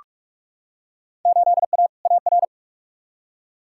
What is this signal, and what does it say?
Morse code, heard as a keyed tone, about 46 wpm, 700 Hz. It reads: 8A AR
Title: Field Day Exchanges - 45wpm